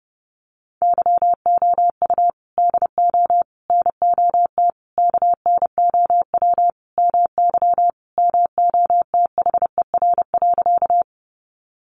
Morse code, heard as a keyed tone, about 30 wpm, 700 Hz. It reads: YOU DO NOT KNOW MY MOTHER.
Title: Sentences from Top 600 Words 30wpm